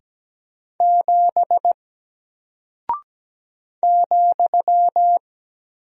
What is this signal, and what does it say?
Morse code, heard as a keyed tone, about 17 wpm, 700 Hz. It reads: 7 ,
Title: Single Character (Letter:Number:Punctuation:Prosign) Alt 17wpm